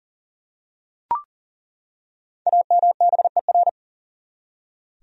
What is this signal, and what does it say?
Morse code, heard as a keyed tone, about 40 wpm, 700 Hz. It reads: AMBER
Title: USA Common Names - Max 5 Letters - 40wpm